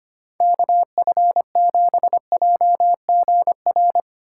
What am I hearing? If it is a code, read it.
Morse code, heard as a keyed tone, about 31 wpm, 700 Hz. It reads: KF7JGR
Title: Speed Racing - USA 2x3 Call Signs 25wpm